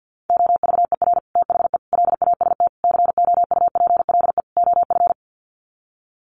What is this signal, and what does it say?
Morse code, heard as a keyed tone, about 50 wpm, 700 Hz. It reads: OVER THE LAST COUPLE OF
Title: Sets of 5 Words Encore 50wpm